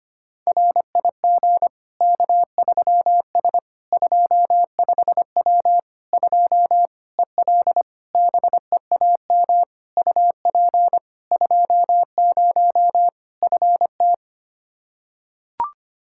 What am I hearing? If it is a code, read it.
Morse code, heard as a keyed tone, about 25 wpm, 700 Hz. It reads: RIG K3S 25W 2 EL BEAM UP 20 FT
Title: Practice QSO - Exchange 2 - Info - 25wpm